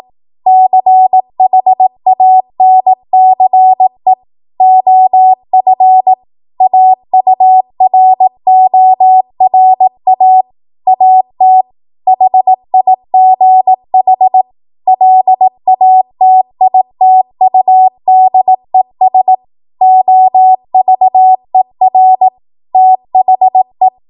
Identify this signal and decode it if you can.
Morse code, heard as a keyed tone, about 18 wpm, 750 Hz. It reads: CHANCE OF AURORA AT HIGH LATITUDES OVER THE
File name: Archive / 18 WPM / 18WPM